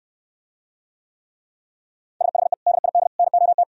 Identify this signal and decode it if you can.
Morse code, heard as a keyed tone, about 66 wpm, 700 Hz. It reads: SHE DID NOT